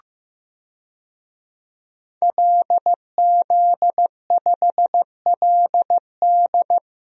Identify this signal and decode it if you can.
Morse code, heard as a keyed tone, about 16 wpm, 700 Hz. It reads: LZ5LD
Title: DXCC-100 Call Signs - 5 Characters - Repeated 2x - 15wpm